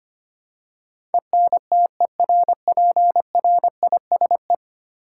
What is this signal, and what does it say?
Morse code, heard as a keyed tone, about 25 wpm, 700 Hz. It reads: ENTERPRISE
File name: Ten Letter Words - 25wpm